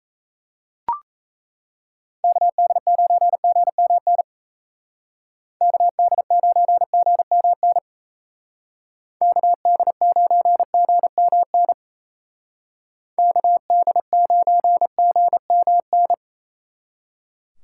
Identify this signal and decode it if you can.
Morse code, heard as a keyed tone, about 35 wpm, 700 Hz. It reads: KD9GMN KD9GMN KD9GMN KD9GMN
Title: Speed Racing - USA 2x3 Call Signs 28wpm